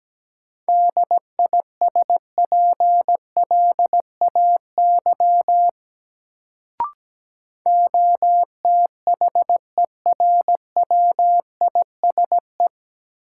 Morse code, heard as a keyed tone, about 17 wpm, 700 Hz. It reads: DISPLAY OTHERWISE
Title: Sentences from Top 1200 Words - New Word Drill Alt - 17wpm